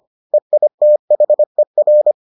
Morse code, heard as a keyed tone, about 25 wpm, 600 Hz. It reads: EITHER